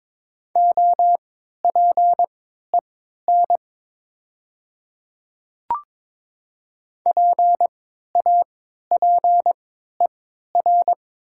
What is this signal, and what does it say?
Morse code, heard as a keyed tone, about 22 wpm, spaced as for 12 wpm, 700 Hz. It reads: OPEN PAPER
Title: 3x Character Spacing - Top 500 Words - 22wpm